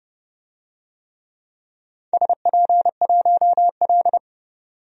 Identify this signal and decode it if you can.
Morse code, heard as a keyed tone, about 30 wpm, 700 Hz. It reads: SP1L
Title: DXCC-100 Call Signs - 4 Characters - 30wpm